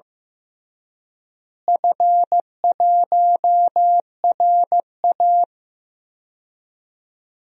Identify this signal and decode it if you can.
Morse code, heard as a keyed tone, about 15 wpm, 700 Hz. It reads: F1RA
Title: DXCC-100 Call Signs - 4 Characters - Repeated 3x - 15wpm